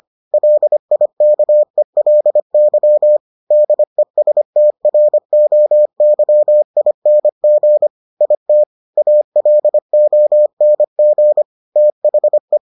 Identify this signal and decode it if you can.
Morse code, heard as a keyed tone, about 25 wpm, 600 Hz. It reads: LIKELY DESTROYING IT ALONG THE